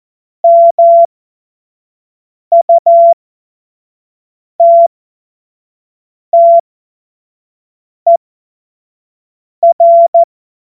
Morse code, heard as a keyed tone, about 14 wpm, spaced as for 5 wpm, 680 Hz. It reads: MUTTER
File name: Morsekurs / Witze / Witz 06. Dezember 2022 25 F70